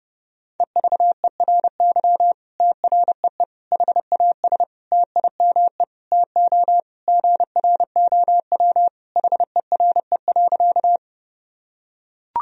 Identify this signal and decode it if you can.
Morse code, heard as a keyed tone, about 30 wpm, 700 Hz. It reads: EVERY TREE HAS TIME TO GROW HERE.
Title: Sentences from Top 1000 Words - Review Part 1 - 30wpm